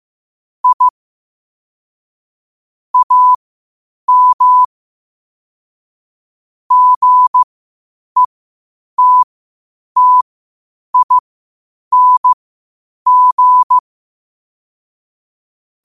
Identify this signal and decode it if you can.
Morse code, heard as a keyed tone, about 15 wpm, spaced as for 8 wpm, 1000 Hz. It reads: I AM GETTING
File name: qotdfiles / 8wpm